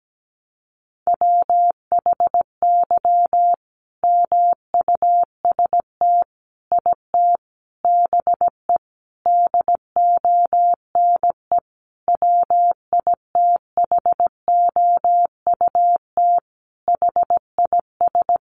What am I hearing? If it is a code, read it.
Morse code, heard as a keyed tone, about 17 wpm, 700 Hz. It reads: WHY MUST IT BE DONE WITHOUT HIS
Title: Sentences from Top 600 Words 17wpm